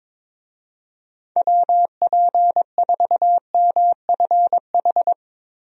Morse code, heard as a keyed tone, about 22 wpm, 700 Hz. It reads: WP4MFH
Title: USA Call Signs - 6 Characters - 22wpm